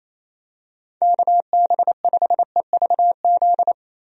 Morse code, heard as a keyed tone, about 28 wpm, 700 Hz. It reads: KB5EVZ
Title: USA Call Signs - 6 Characters - Repeated 2x - 28wpm